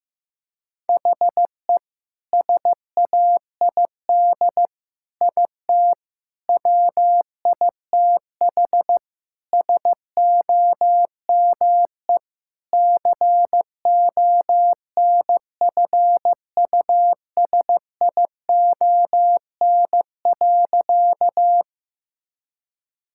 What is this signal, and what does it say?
Morse code, heard as a keyed tone, about 15 wpm, 700 Hz. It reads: HE SAID IT WITH SOME CONFUSION.
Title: Sentences from Top 1600 Words 15wpm